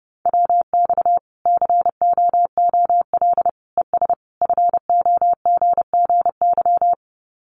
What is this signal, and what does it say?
Morse code, heard as a keyed tone, about 30 wpm, 700 Hz. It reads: WX COOL ES FOGGY
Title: Common QSO Phrases 30wpm